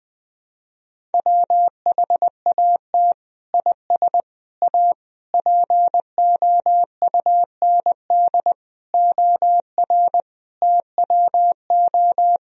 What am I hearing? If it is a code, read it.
Morse code, heard as a keyed tone, about 20 wpm, 700 Hz. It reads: WHAT IS A POUND OR TWO
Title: Sentences from Top 1000 Words - Review Part 2 - 20wpm